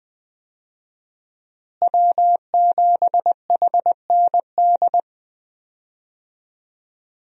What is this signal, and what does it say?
Morse code, heard as a keyed tone, about 21 wpm, 700 Hz. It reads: W7HND
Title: USA Call Signs - 5 Characters - Repeated 2x - 20wpm